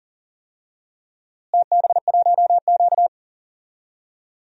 Morse code, heard as a keyed tone, about 39 wpm, 700 Hz. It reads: TB1Q
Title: DXCC-100 Call Signs - 4 Characters - Repeated 3x - 40wpm